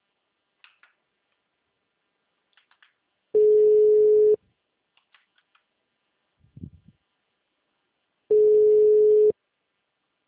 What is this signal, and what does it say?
Ringback tone (Europe, 425 Hz).
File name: out- -2011- - - .397